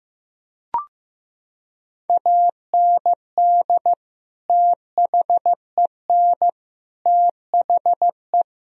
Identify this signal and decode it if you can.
Morse code, heard as a keyed tone, about 15 wpm, 700 Hz. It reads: AND THEN THE
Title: Sentences from Top 1000 Words - Review Part 3 - 15wpm